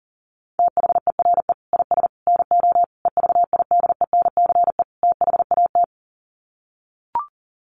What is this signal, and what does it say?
Morse code, heard as a keyed tone, about 40 wpm, 700 Hz. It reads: THERE IS NO EVIDENCE THAT
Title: Sets of 5 Words Encore 40wpm